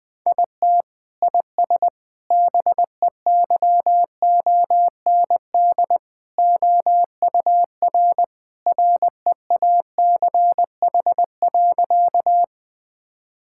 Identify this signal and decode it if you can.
Morse code, heard as a keyed tone, about 20 wpm, 700 Hz. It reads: IT IS BEYOND OUR REACH.
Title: Sentences from Top 1000 Words - Review Part 6 - 20wpm